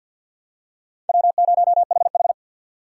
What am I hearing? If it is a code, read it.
Morse code, heard as a keyed tone, about 47 wpm, 700 Hz. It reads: W0HH
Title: USA Call Signs - 4 Characters - 6x Character Spacing - 50wpm